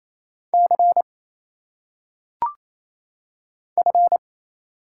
Morse code, heard as a keyed tone, about 28 wpm, 700 Hz. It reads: C F
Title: Single Letters - Phonetic Pronunciation 28wpm